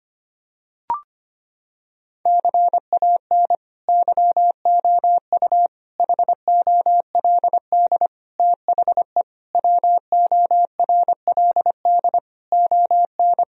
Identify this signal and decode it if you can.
Morse code, heard as a keyed tone, about 25 wpm, 700 Hz. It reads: CAN YOU HOLD THE WORLD ON
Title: Sentences from Top 1000 Words - Review Part 2 - 25wpm